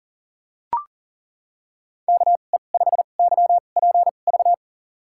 Morse code, heard as a keyed tone, about 40 wpm, spaced as for 29 wpm, 700 Hz. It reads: KE5YPV
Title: USA Call Signs - 6 Characters - 2x Character Spacing - 40wpm